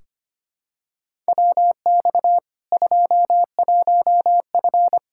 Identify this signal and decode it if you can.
Morse code, heard as a keyed tone, about 25 wpm, 700 Hz. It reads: WX 21F
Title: K1USN SST QSO Practice 25wpm